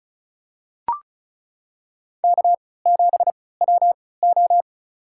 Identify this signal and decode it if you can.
Morse code, heard as a keyed tone, about 34 wpm, spaced as for 20 wpm, 700 Hz. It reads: K7WO
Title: USA Call Signs - 3x Character Spacing - 35wpm